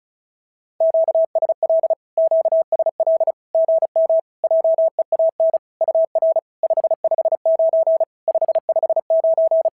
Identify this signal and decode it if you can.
Morse code, heard as a keyed tone, about 36 wpm, 650 Hz. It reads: QSL QSL GM JEAN UR 559 559
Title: SOTA QSO Practice 35wpm